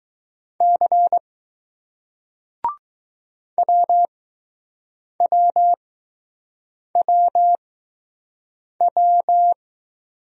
Morse code, heard as a keyed tone, about 21 wpm, spaced as for 6 wpm, 700 Hz. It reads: C WWWW